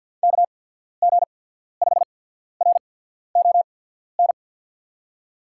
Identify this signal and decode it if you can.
Morse code, heard as a keyed tone, about 48 wpm, spaced as for 14 wpm, 700 Hz. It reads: KG5RON
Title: USA Call Signs - 6 Characters - 8x Character Spacing - 50wpm